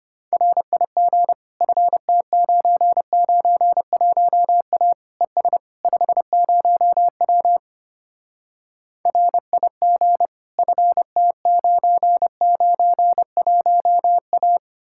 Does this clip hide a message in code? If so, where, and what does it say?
Yes: Morse code from 0.3 to 14.6 s, heard as a keyed tone, about 33 wpm, 700 Hz. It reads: RIG FT991A ES 50W RIG FT991A